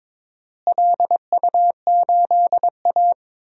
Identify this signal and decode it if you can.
Morse code, heard as a keyed tone, about 22 wpm, 700 Hz. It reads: LU8A
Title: DXCC-100 Call Signs - 4 Characters - Repeated 3x - 22wpm